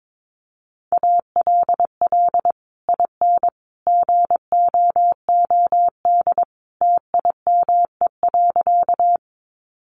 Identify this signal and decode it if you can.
Morse code, heard as a keyed tone, about 22 wpm, 700 Hz. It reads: ALL IN GOOD TIME.
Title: Sentences from Top 200 Words 22wpm